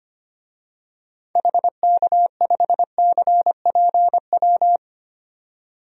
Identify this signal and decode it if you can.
Morse code, heard as a keyed tone, about 25 wpm, 700 Hz. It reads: HK5CPW
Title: DXCC-100 Call Signs - 6 Characters - 25wpm